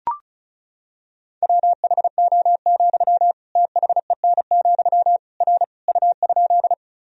Morse code, heard as a keyed tone, about 36 wpm, 700 Hz. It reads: WHO, THEN, R U?
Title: Sentences from Top 100 Words and Abbreviations 35wpm